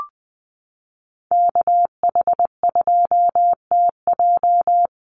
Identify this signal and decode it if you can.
Morse code, heard as a keyed tone, about 20 wpm, 700 Hz. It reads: KH2TJ
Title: NA SOTA Activator Callsigns 20wpm